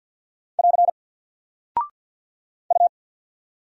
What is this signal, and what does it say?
Morse code, heard as a keyed tone, about 50 wpm, 700 Hz. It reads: <AR> U
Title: Single Character (Letter:Number:Punctuation:Prosign) 50wpm